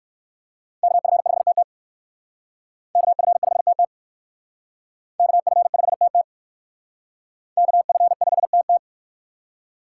Morse code, heard as a keyed tone, about 56 wpm, 700 Hz. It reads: KF5TT KF5TT KF5TT KF5TT
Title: Speed Racing - USA 2x2 Call Signs 45wpm